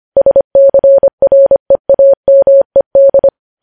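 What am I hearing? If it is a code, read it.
Morse code, heard as a keyed tone, about 25 wpm, 550 Hz. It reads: SCREAMED
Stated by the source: News headlines sent in Morse at 25 wpm.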